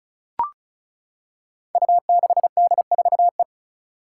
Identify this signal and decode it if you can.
Morse code, heard as a keyed tone, about 36 wpm, 700 Hz. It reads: U6D4E
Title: Five Challenging Letter-Number Combinations 35wpm